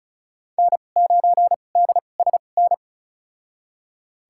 Morse code, heard as a keyed tone, about 36 wpm, spaced as for 25 wpm, 700 Hz. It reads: N9DSN
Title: USA Call Signs - 5 Characters - 2x Character Spacing - 35wpm